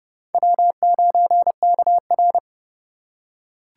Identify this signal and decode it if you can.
Morse code, heard as a keyed tone, about 28 wpm, 690 Hz. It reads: W9KR